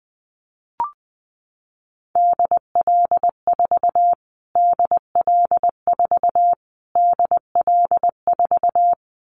Morse code, heard as a keyed tone, about 20 wpm, 700 Hz. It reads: DL4 DL4 DL4
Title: Callsign Prefixes for Top 25 DXCC - Repeated 3x - 20wpm